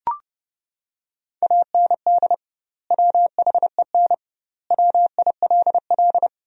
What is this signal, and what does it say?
Morse code, heard as a keyed tone, about 30 wpm, 700 Hz. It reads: AND WHEN WILL
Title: 2x Word-Spacing - Sentences from Top 100 Words - 30wpm